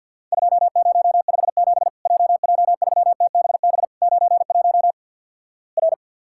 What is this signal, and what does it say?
Morse code, heard as a keyed tone, about 48 wpm, 700 Hz. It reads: 2057 JJ3TBB 91 R
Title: Worked All Europe DX Contest - QTC Exchange - 50wpm